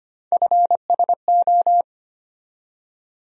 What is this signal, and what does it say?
Morse code, heard as a keyed tone, about 25 wpm, 700 Hz. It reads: FSO
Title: USA Call Sign Postfixes - 3 Letters 25wpm